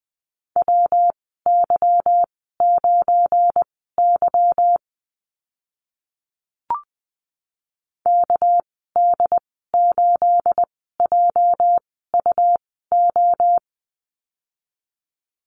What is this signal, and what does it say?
Morse code, heard as a keyed tone, about 20 wpm, spaced as for 15 wpm, 700 Hz. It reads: WY9Y KD8JUO
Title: 219 Lesson 35g - Number 0 - Call Signs 20wpm